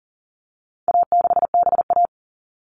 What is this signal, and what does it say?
Morse code, heard as a keyed tone, about 40 wpm, 700 Hz. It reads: A6BA
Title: All Cut Numbers - 4-Digit 40wpm